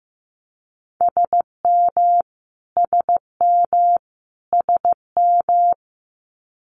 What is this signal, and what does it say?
Morse code, heard as a keyed tone, about 15 wpm, 700 Hz. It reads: SM SM SM
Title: Multi-letter Phonemes Repeated 3x 15wpm